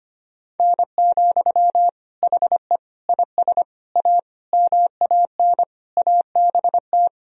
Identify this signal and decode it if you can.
Morse code, heard as a keyed tone, about 25 wpm, 700 Hz. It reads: N, HE IS A MAN ABT
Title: Sentences from Top 400 Words and Abbreviations 25wpm